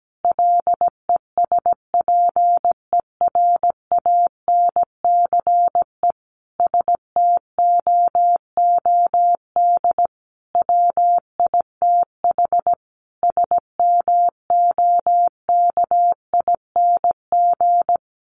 Morse code, heard as a keyed tone, about 17 wpm, 700 Hz. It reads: LESPERANCE STOOD WITH SMOKING
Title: A Sound Of Thunder 17wpm